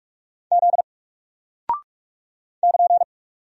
Morse code, heard as a keyed tone, about 44 wpm, 700 Hz. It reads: Z <KN>
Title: Single Character (Letter:Number:Punctuation:Prosign) 45wpm